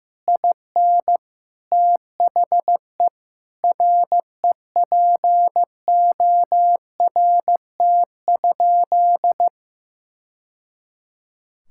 Morse code, heard as a keyed tone, about 15 wpm, 700 Hz. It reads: IN THE REPORT?
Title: Sentences from Top 1000 Words - Review Part 6 - 15wpm